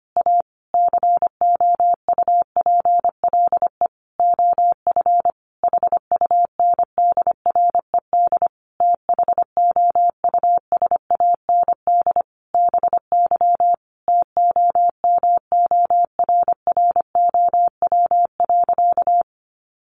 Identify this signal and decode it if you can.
Morse code, heard as a keyed tone, about 25 wpm, 700 Hz. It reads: A COUPLE OF HUNDRED THOUSAND BY TOMORROW.
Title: Sentences from Top 1100 Words 25wpm